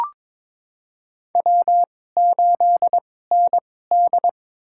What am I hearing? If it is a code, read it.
Morse code, heard as a keyed tone, about 22 wpm, spaced as for 16 wpm, 700 Hz. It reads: W8ND
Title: USA Call Signs - 4 Characters - 2x Character Spacing - 22wpm